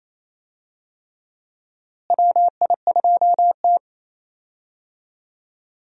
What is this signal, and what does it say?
Morse code, heard as a keyed tone, about 28 wpm, 700 Hz. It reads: WI2T
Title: USA Call Signs - 4 Characters - Repeated 3x - 28wpm